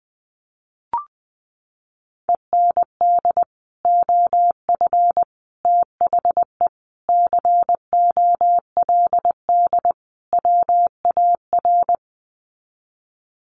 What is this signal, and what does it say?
Morse code, heard as a keyed tone, about 20 wpm, 700 Hz. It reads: END OF THE COLD WAR
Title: Sets of 5 Words Encore 20wpm